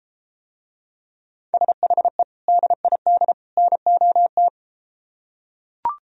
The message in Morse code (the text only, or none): SHE DID NOT